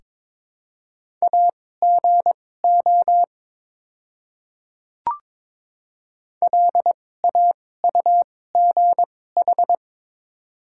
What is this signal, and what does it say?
Morse code, heard as a keyed tone, about 22 wpm, spaced as for 16 wpm, 700 Hz. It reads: AGO LAUGH
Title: 2x Character Spacing - Top 500 Words - 22wpm